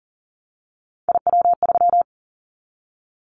Morse code, heard as a keyed tone, about 40 wpm, 700 Hz. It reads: IW3
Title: Callsign Prefixes for Top 25 DXCC - Repeated 3x - 40wpm